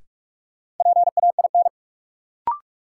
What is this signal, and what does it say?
Morse code, heard as a keyed tone, about 46 wpm, 700 Hz. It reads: PAIN
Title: Sentences from Top 1600 Words - New Word Drill - 45wpm